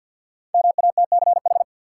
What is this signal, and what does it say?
Morse code, heard as a keyed tone, about 52 wpm, 700 Hz. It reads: MATCH
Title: Five Letter Words - 50wpm